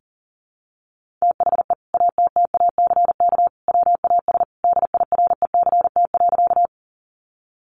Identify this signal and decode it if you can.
Morse code, heard as a keyed tone, about 37 wpm, 700 Hz. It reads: THE ATTACK WAS DIRECT.
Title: Sentences from Top 1500 Words 40wpm